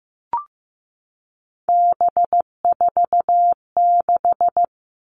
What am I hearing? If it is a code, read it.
Morse code, heard as a keyed tone, about 15 wpm, 700 Hz. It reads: B46
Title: All Cut Numbers - 3-Digit 15wpm